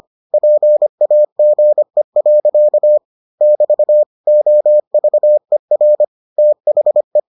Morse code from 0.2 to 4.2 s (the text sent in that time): PAGE. <BT>